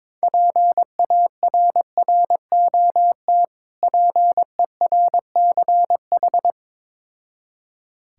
Morse code, heard as a keyed tone, about 22 wpm, 690 Hz. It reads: PARROT PERCH
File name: Difficult Words with Hints - 22wpm